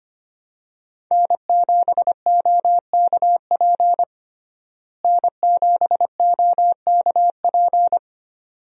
Morse code, heard as a keyed tone, about 25 wpm, 700 Hz. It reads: N7OKP N7OKP
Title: USA Call Signs - 5 Characters - Repeated 2x - 25wpm